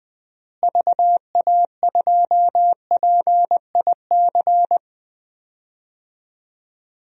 Morse code, heard as a keyed tone, about 20 wpm, 700 Hz. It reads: VA2PIC
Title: DXCC-100 Call Signs - 6 Characters - Repeated 3x - 20wpm